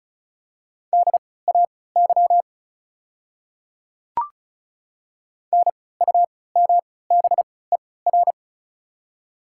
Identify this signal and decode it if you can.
Morse code, heard as a keyed tone, about 36 wpm, spaced as for 20 wpm, 690 Hz. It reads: DAY NUMBER
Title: 3x Character Spacing - Top 100 Words - 35wpm